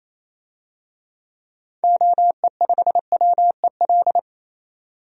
Morse code, heard as a keyed tone, about 28 wpm, 700 Hz. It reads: OE5WEL
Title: DXCC-100 Call Signs - 6 Characters - 28wpm